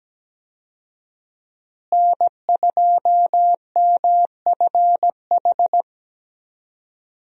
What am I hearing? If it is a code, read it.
Morse code, heard as a keyed tone, about 17 wpm, 700 Hz. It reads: N2MFH
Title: USA Call Signs - 5 Characters - Repeated 2x - 17wpm